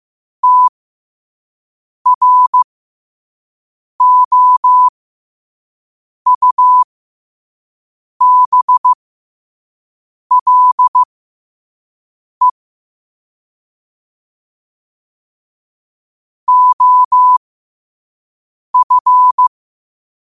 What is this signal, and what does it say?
Morse code, heard as a keyed tone, about 15 wpm, spaced as for 5 wpm, 1000 Hz. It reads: TROUBLE OF